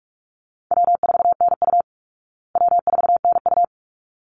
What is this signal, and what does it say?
Morse code, heard as a keyed tone, about 44 wpm, 700 Hz. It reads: W4NU W4NU
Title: Easy Call Signs Repeated 2x - 45wpm